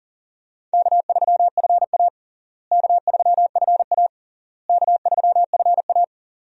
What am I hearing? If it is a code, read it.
Morse code, heard as a keyed tone, about 40 wpm, 700 Hz. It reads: K3FA K3FA K3FA
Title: USA Call Signs - 4 Characters - Repeated 3x - 40wpm